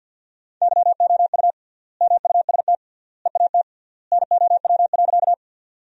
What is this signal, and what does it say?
Morse code, heard as a keyed tone, about 50 wpm, 700 Hz. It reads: YOU MUST EAT NOW.
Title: 3x Word-Spacing - Sentences from Top 300 Words - 50wpm